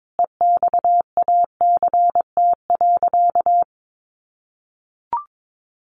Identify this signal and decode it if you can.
Morse code, heard as a keyed tone, about 22 wpm, 700 Hz. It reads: EXACT.
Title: Sentences from Top 1300 Words 22wpm